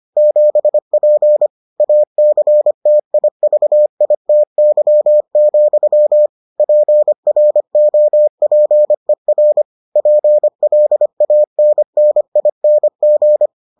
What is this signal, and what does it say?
Morse code, heard as a keyed tone, about 25 wpm, 600 Hz. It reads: 7P ACTIVITY, PROPER PLANNING